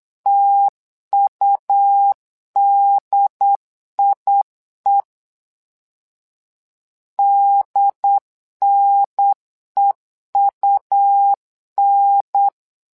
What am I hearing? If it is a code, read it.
Morse code, heard as a keyed tone, about 8 wpm, 800 Hz. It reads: TUDIE DNEUN
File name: LEZIONI / 2 UHDN / 2 UHDN 8wpm2232